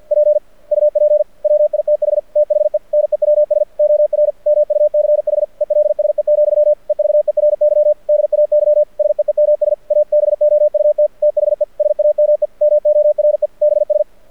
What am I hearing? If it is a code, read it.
Morse code, heard as a keyed tone, about 53 wpm, 600 Hz. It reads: P UP WITH THE NEWS OR MUCH ELSE, EVERY DAY SEEMS ABOUT THE SAME MORE BS
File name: archived / CRSnet